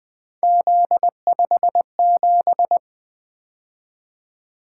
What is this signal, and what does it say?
Morse code, heard as a keyed tone, about 20 wpm, 700 Hz. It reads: Z57
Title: Three Letter-Number Combinations - 20wpm